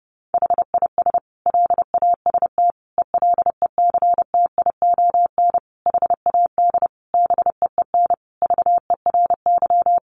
Morse code, heard as a keyed tone, about 28 wpm, 700 Hz. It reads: HIS LAST ELECTION HAD BEEN VERY
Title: Sentences from Top 1500 Words 30wpm